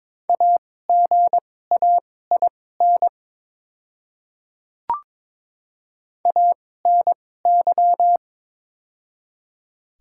Morse code, heard as a keyed tone, about 22 wpm, spaced as for 16 wpm, 700 Hz. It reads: AGAIN ANY
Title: 2x Character Spacing - Top 300 Words - 22wpm